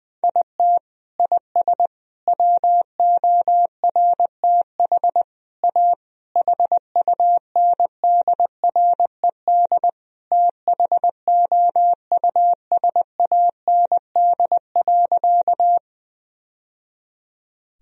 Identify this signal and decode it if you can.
Morse code, heard as a keyed tone, about 21 wpm, 700 Hz. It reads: IT IS WORTH A HUNDRED THOUSAND.
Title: Sentences from Top 1000 Words - Review Part 3 - 20wpm